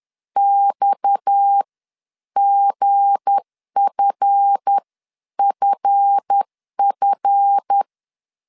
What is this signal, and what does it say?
Morse code, heard as a keyed tone, about 10 wpm, spaced as for 9 wpm, 800 Hz. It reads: XGFFF
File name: LEZIONI / 7 GF9Z / 7 GF 10wpm 2514